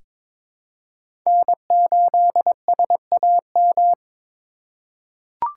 Morse code, heard as a keyed tone, about 22 wpm, 700 Hz. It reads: N8SAM
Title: USA Call Signs - 5 Characters - 4x Character Spacing - 22wpm